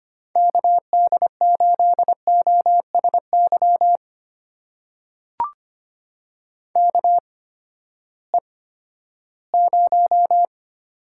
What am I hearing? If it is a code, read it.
Morse code, heard as a keyed tone, about 25 wpm, spaced as for 14 wpm, 700 Hz. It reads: KD8OSY KE0
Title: USA Call Signs - 6 Characters - 8x Character Spacing - 25wpm